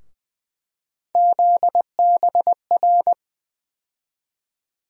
Morse code, heard as a keyed tone, about 20 wpm, 700 Hz. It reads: ZBR
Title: USA Call Sign Postfixes - 3 Letters 20wpm